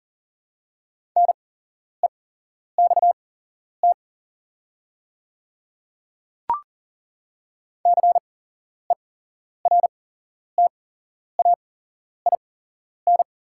Morse code, heard as a keyed tone, about 40 wpm, spaced as for 11 wpm, 690 Hz. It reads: NEXT CERTAIN
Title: 8x Character Spacing - Top 500 Words - 40wpm